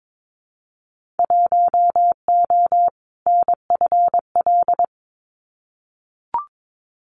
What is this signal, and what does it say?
Morse code, heard as a keyed tone, about 22 wpm, 700 Hz. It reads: 1O NFL
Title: Winter Field Day Exchanges 22wpm